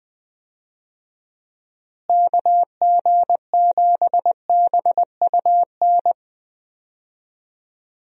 Morse code, heard as a keyed tone, about 21 wpm, 700 Hz. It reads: KG7BUN
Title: USA Call Signs - 6 Characters - Repeated 2x - 20wpm